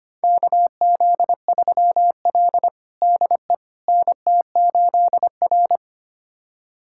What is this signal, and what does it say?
Morse code, heard as a keyed tone, about 25 wpm, 700 Hz. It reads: KZ3L DE NT8R
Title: USA Callsign DE Callsign - 4 Characters - 25wpm